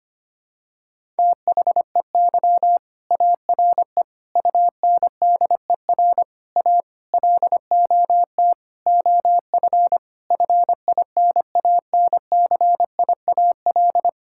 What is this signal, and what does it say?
Morse code, heard as a keyed tone, about 25 wpm, 700 Hz. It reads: THEY ARE UNDER A LOT OF FINANCIAL
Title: Sentences from Top 1000 Words - Review Part 6 - 25wpm